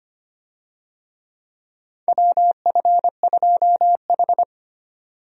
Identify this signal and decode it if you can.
Morse code, heard as a keyed tone, about 25 wpm, 700 Hz. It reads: WF2H
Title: USA Call Signs - 4 Characters - 25wpm